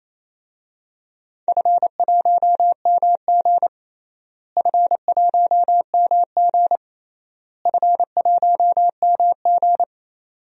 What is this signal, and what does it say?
Morse code, heard as a keyed tone, about 28 wpm, 700 Hz. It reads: F1MG F1MG F1MG
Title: DXCC-100 Call Signs - 4 Characters - Repeated 3x - 28wpm